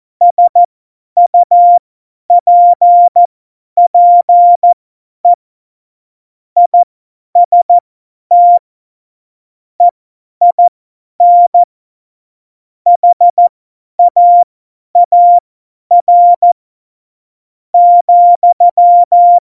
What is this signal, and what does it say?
Morse code, heard as a keyed tone, about 14 wpm, spaced as for 9 wpm, 700 Hz. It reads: SUPPE IST EIN HAAR,